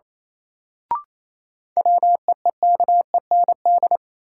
Morse code, heard as a keyed tone, about 27 wpm, 700 Hz. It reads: WEEKEND
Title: Sets of 5 Words - New Word Drill - 28wpm